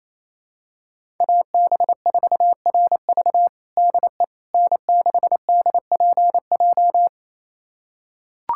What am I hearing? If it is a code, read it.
Morse code, heard as a keyed tone, about 28 wpm, 690 Hz. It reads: AB4RV DE N6DPJ
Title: USA Callsign DE Callsign - 5 Characters - 28wpm